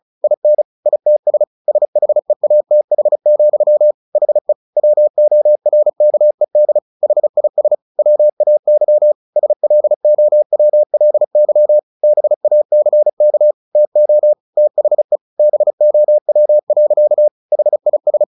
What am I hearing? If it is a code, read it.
Morse code, heard as a keyed tone, about 34 wpm, 600 Hz. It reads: IN ITS SHEATH, HE WORKED HIS WAY SLOWLY BACK TO THE BOW. HIS
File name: the old man and the sea 2 35wpm